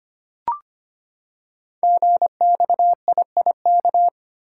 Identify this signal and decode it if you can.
Morse code, heard as a keyed tone, about 25 wpm, 700 Hz. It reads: GXIIK
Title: Code Groups - 5 Letters - 25wpm